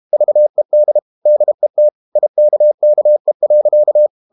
Morse code, heard as a keyed tone, about 32 wpm, 600 Hz. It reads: VED DET IKKE.
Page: Ferie 32 wpm